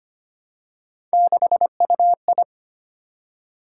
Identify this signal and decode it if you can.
Morse code, heard as a keyed tone, about 25 wpm, 700 Hz. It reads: 6UI
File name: Three Letter-Number Combinations - 25wpm